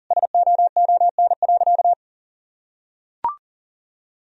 Morse code, heard as a keyed tone, about 40 wpm, 700 Hz. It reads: SOON.
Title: Sentences from Top 1000 Words - Review Part 4 - 40wpm